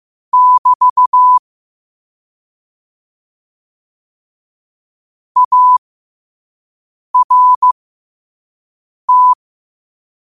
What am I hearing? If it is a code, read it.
Morse code, heard as a keyed tone, about 15 wpm, spaced as for 5 wpm, 1000 Hz. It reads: <BT> ART
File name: qotdfiles / 5wpm